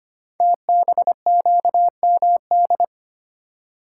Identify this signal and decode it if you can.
Morse code, heard as a keyed tone, about 25 wpm, 700 Hz. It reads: TBQMD
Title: Code Groups - 5 Letters - 25wpm